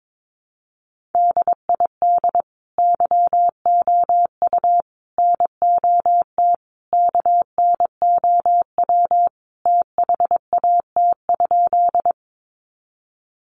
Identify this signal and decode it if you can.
Morse code, heard as a keyed tone, about 22 wpm, 700 Hz. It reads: DID YOU NOT KNOW THAT?
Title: Sentences from Top 600 Words 22wpm